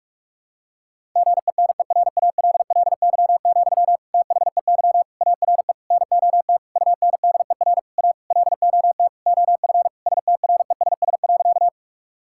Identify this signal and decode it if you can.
Morse code, heard as a keyed tone, about 42 wpm, 700 Hz. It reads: GENERALLY, THEY ARE NOT UNDER A LOT OF STRESS.
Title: Sentences from Top 1000 Words - Review Part 6 - 45wpm